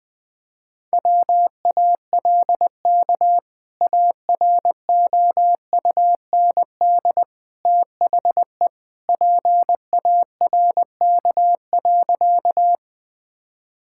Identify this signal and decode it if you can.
Morse code, heard as a keyed tone, about 20 wpm, 700 Hz. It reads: WALK AROUND THE PARK.
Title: Sentences with 2-Word Phrases - 20wpm